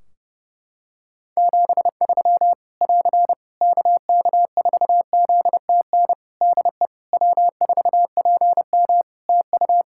Morse code, heard as a keyed tone, about 30 wpm, 700 Hz. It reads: 73 <AR> KK4ZTN DE W4PM TU
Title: Practice QSO - Exchange 4 - Ending - 30wpm